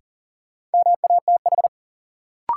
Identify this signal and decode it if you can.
Morse code, heard as a keyed tone, about 40 wpm, 700 Hz. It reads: MATH
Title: Sentences from Top 1000 Words - Review Part 6 - New Word Drill - 40wpm